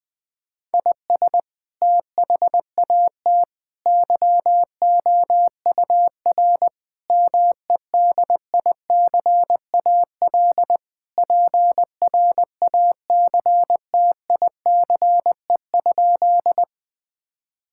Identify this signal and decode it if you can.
Morse code, heard as a keyed tone, about 20 wpm, 700 Hz. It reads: IS THAT YOUR MEDICAL PRACTICE?
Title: Sentences from Top 1000 Words - Review Part 5 - 20wpm